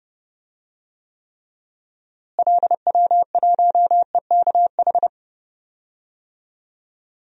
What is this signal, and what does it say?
Morse code, heard as a keyed tone, about 31 wpm, 700 Hz. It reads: LW1EKH
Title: DXCC-100 Call Signs - 6 Characters - Repeated 2x - 30wpm